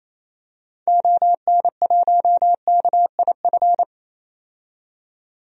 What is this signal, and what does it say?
Morse code, heard as a keyed tone, about 28 wpm, 700 Hz. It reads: ON1KIF
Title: DXCC-100 Call Signs - 6 Characters - 28wpm